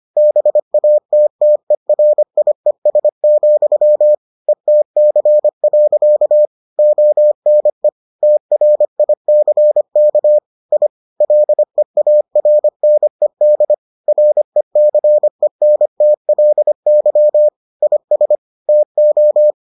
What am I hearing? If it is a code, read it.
Morse code, heard as a keyed tone, about 25 wpm, 600 Hz. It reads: BATTERIES, ETC. ONE TRICK I LEARNED RECENTLY IS TO